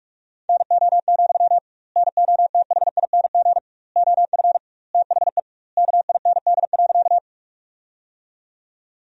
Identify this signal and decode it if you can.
Morse code, heard as a keyed tone, about 47 wpm, 700 Hz. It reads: NO, NOTHING OF THE KIND.
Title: 2x Word-Spacing - Sentences from Top 500 Words - 45wpm